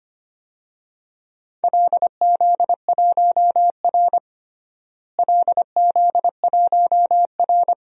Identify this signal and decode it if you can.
Morse code, heard as a keyed tone, about 25 wpm, 700 Hz. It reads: LZ1R LZ1R
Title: DXCC-100 Call Signs - 4 Characters - Repeated 3x - 25wpm